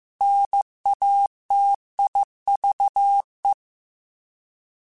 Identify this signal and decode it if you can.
Morse code, heard as a keyed tone, about 15 wpm, 800 Hz. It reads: NATIVE
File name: Alphabetical / Alphabetical mp3s / Ns @13